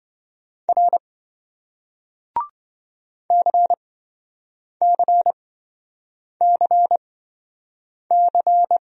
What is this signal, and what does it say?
Morse code, heard as a keyed tone, about 27 wpm, spaced as for 7 wpm, 700 Hz. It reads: R CCCC